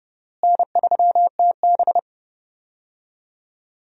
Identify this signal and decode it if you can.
Morse code, heard as a keyed tone, about 30 wpm, 700 Hz. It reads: N3TB
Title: USA Call Signs - 4 Characters - Repeated 3x - 30wpm